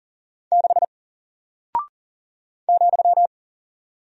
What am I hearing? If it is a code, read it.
Morse code, heard as a keyed tone, about 40 wpm, 700 Hz. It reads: 6 ,
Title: Single Character (Letter:Number:Punctuation:Prosign) Alt 40wpm